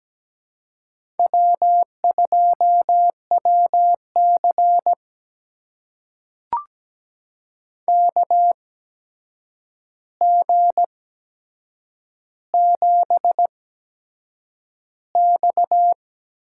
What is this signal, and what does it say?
Morse code, heard as a keyed tone, about 17 wpm, spaced as for 7 wpm, 700 Hz. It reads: W2WC KG7X
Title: USA Call Signs - 4 Characters - 8x Character Spacing - 17wpm